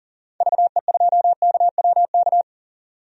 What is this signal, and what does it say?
Morse code, heard as a keyed tone, about 40 wpm, 700 Hz. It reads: VE2KWK
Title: DXCC-100 Call Signs - 6 Characters - 40wpm